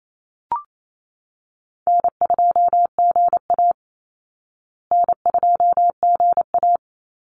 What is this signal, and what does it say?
Morse code, heard as a keyed tone, about 28 wpm, 700 Hz. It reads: N2GA N2GA
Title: Easy Call Signs Repeated 3x - 28wpm